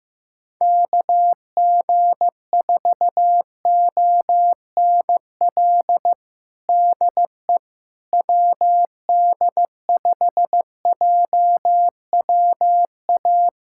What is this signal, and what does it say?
Morse code, heard as a keyed tone, about 15 wpm, 700 Hz. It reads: KG4ONL DE WD5JWA
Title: USA Callsign DE Callsign - 6 Characters - 15wpm